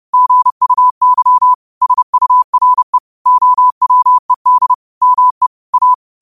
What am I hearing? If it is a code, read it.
Morse code, heard as a keyed tone, about 30 wpm, 1000 Hz. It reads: GUY SURE OWED ME A